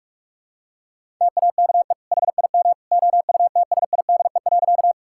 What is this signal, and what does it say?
Morse code, heard as a keyed tone, about 46 wpm, 700 Hz. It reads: TAKE HIM OUTSIDE.
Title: Sentences from Top 1000 Words - Review Part 6 - 45wpm